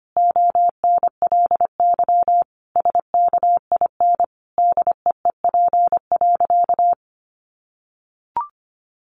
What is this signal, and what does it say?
Morse code, heard as a keyed tone, about 25 wpm, 700 Hz. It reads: ONLY SKIN DEEP.
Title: Sentences from Top 1300 Words 25wpm